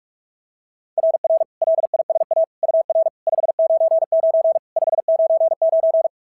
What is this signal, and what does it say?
Morse code, heard as a keyed tone, about 44 wpm, 690 Hz. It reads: RR LISA UR 599 599
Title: SOTA QSO Practice 45wpm